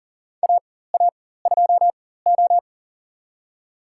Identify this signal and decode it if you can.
Morse code, heard as a keyed tone, about 40 wpm, spaced as for 19 wpm, 700 Hz. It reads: AA2O
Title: USA Call Signs - 4x Character Spacing - 40wpm